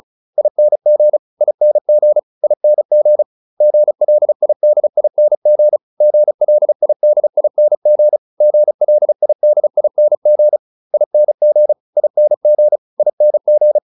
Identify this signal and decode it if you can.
Morse code, heard as a keyed tone, about 34 wpm, 600 Hz. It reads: ING ING ING GLIDING GLIDING GLIDING ING ING ING